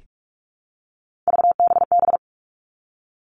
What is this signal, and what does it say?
Morse code, heard as a keyed tone, about 46 wpm, 700 Hz. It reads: VBB
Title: Three Challenging Letter-Number Combinations 45wpm